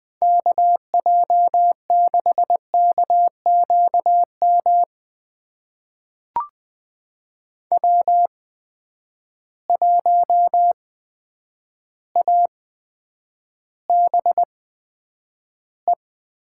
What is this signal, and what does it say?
Morse code, heard as a keyed tone, about 20 wpm, spaced as for 9 wpm, 700 Hz. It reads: KJ6KQM W1ABE
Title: USA Call Signs - 8x Character Spacing - 20wpm